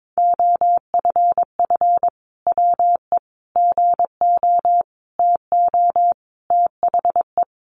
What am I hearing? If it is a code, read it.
Morse code, heard as a keyed tone, about 22 wpm, 700 Hz. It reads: OFF WE GO TO THE
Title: Sentences from Top 1400 Words 22wpm